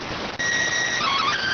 SSTV (slow-scan television), PD 290.